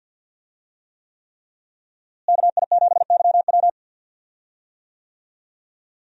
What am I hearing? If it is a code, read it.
Morse code, heard as a keyed tone, about 52 wpm, 700 Hz. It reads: KI7YW
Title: USA Call Signs - 5 Characters - Repeated 3x - 50wpm